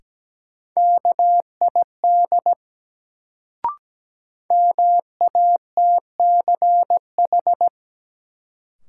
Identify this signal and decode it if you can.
Morse code, heard as a keyed tone, about 17 wpm, 700 Hz. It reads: KID MATCH
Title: Sentences from Top 1600 Words - New Word Drill Alt - 17wpm